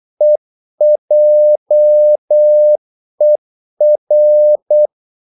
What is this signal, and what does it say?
Morse code, heard as a keyed tone, about 8 wpm, 600 Hz. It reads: EJER
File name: Korte ord DK 8wpm